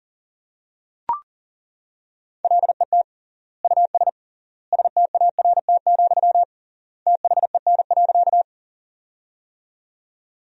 Morse code, heard as a keyed tone, about 40 wpm, 700 Hz. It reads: LET US START, THEN.
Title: 3x Word-Spacing - Sentences from Top 300 Words - 40wpm